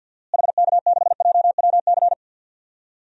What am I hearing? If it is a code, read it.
Morse code, heard as a keyed tone, about 48 wpm, 700 Hz. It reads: HK6JWC
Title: DXCC-100 Call Signs - 6 Characters - Repeated 3x - 50wpm